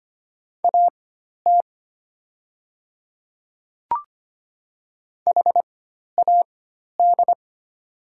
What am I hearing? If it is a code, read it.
Morse code, heard as a keyed tone, about 25 wpm, spaced as for 12 wpm, 700 Hz. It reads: AT HAD
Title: 4x Character Spacing - Top 100 Words - 25wpm